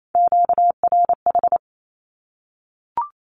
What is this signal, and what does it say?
Morse code, heard as a keyed tone, about 28 wpm, 700 Hz. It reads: QRH
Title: Q-Codes 28wpm